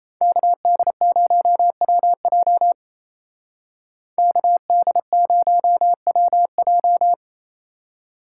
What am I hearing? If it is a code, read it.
Morse code, heard as a keyed tone, about 36 wpm, 700 Hz. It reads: KD0WJ KD0WJ